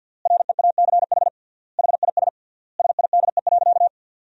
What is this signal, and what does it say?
Morse code, heard as a keyed tone, about 49 wpm, 700 Hz. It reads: REACH HIS SIDE.
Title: 3x Word-Spacing - Sentences from Top 400 Words - 50wpm